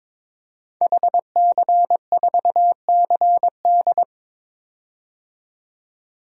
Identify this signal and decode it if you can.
Morse code, heard as a keyed tone, about 22 wpm, 700 Hz. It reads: HC4CD
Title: DXCC-100 Call Signs - 5 Characters - 22wpm